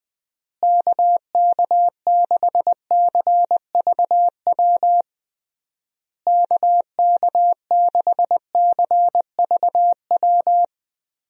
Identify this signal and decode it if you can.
Morse code, heard as a keyed tone, about 20 wpm, 700 Hz. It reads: KK6CVW KK6CVW
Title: USA Call Signs - 6 Characters - Repeated 3x - 20wpm